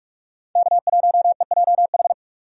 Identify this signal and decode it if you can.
Morse code, heard as a keyed tone, about 43 wpm, 700 Hz. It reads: K1EJH
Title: USA Call Signs - 5 Characters - 45wpm